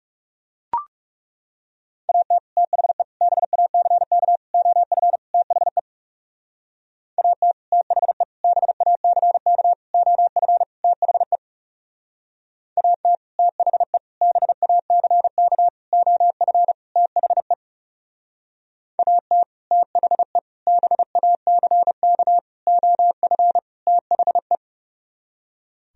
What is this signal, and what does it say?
Morse code, heard as a keyed tone, about 37 wpm, 700 Hz. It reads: AT THE BACK OF THE AT THE BACK OF THE AT THE BACK OF THE AT THE BACK OF THE
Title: Speed-Racing - Sets of 5 Words 30wpm